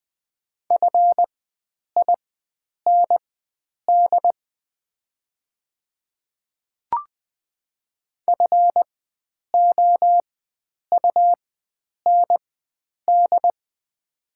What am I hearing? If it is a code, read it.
Morse code, heard as a keyed tone, about 20 wpm, spaced as for 9 wpm, 700 Hz. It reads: FIND FOUND
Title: 4x Character Spacing - Top 300 Words - 20wpm